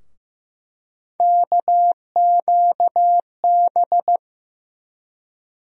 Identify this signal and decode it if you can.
Morse code, heard as a keyed tone, about 15 wpm, 700 Hz. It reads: KQB
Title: USA Call Sign Postfixes - 3 Letters 15wpm